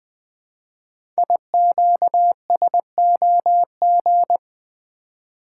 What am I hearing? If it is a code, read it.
Morse code, heard as a keyed tone, about 20 wpm, 700 Hz. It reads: IQSOG
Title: Code Groups - 5 Letters - 20wpm